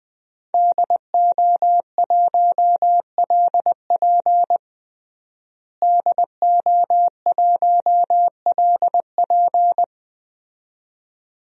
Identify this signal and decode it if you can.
Morse code, heard as a keyed tone, about 21 wpm, 700 Hz. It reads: DO1LP DO1LP
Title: DXCC-100 Call Signs - 5 Characters - Repeated 2x - 20wpm